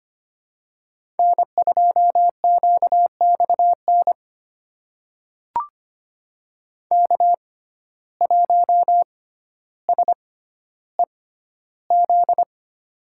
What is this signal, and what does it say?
Morse code, heard as a keyed tone, about 25 wpm, spaced as for 13 wpm, 700 Hz. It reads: N2QXN K1SEZ
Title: USA Call Signs - 5 Characters - 6x Character Spacing - 25wpm